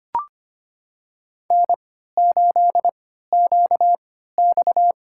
Morse code, heard as a keyed tone, about 25 wpm, spaced as for 14 wpm, 700 Hz. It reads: N8QX
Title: USA Call Signs - 4 Characters - 3x Character Spacing - 25wpm